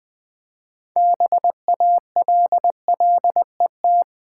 Morse code, heard as a keyed tone, about 20 wpm, 700 Hz. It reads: BALLET
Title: Build A Word - 2 Letters At A Time - 6-Letter Words - 20wpm